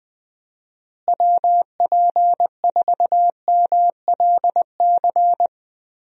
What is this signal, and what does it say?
Morse code, heard as a keyed tone, about 21 wpm, 700 Hz. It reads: WP4MLC
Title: USA Call Signs - 6 Characters - Repeated 2x - 20wpm